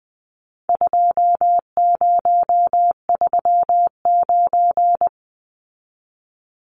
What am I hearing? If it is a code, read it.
Morse code, heard as a keyed tone, about 20 wpm, 700 Hz. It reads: 2039
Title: Numbers - 4-Digit 20wpm